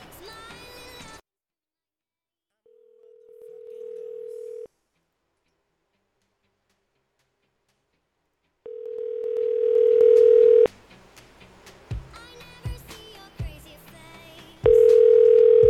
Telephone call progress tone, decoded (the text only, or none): ringback tone